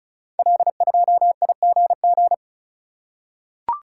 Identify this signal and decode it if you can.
Morse code, heard as a keyed tone, about 36 wpm, 700 Hz. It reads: L2IGG
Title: Code Groups - 5 Letter:Numbers - 35wpm